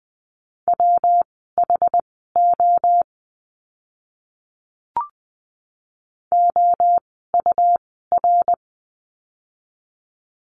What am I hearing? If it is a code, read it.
Morse code, heard as a keyed tone, about 20 wpm, spaced as for 15 wpm, 700 Hz. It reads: WHO OUR
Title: 115 Lesson 19f - Letter w - Words 20wpm